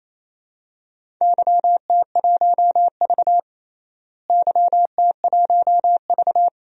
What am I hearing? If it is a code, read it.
Morse code, heard as a keyed tone, about 28 wpm, 700 Hz. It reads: YT1V YT1V
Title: DXCC-100 Call Signs - 4 Characters - Repeated 2x - 28wpm